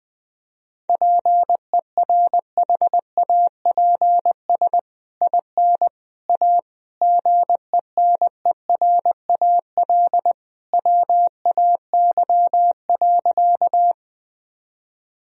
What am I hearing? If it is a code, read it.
Morse code, heard as a keyed tone, about 21 wpm, 700 Hz. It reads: PERHAPS IN A GENERAL WAY.
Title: Sentences from Top 1000 Words - Review Part 3 - 20wpm